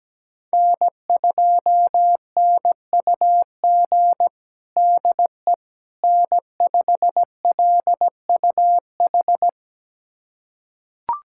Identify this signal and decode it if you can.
Morse code, heard as a keyed tone, about 17 wpm, 700 Hz. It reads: N2NUG DE N5LUH
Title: USA Callsign DE Callsign - 5 Characters - 17wpm